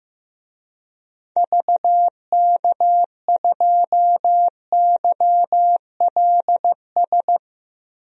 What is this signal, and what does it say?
Morse code, heard as a keyed tone, about 15 wpm, 700 Hz. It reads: VK2YLS
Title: DXCC-100 Call Signs - 6 Characters - 15wpm